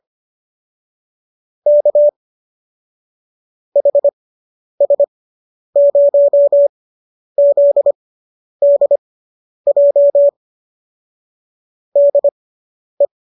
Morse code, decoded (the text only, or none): K HS0ZDJ DE